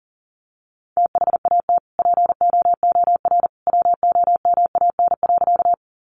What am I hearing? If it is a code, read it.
Morse code, heard as a keyed tone, about 37 wpm, 700 Hz. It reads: THAT POOR WOMAN.
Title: Sentences from Top 1100 Words 40wpm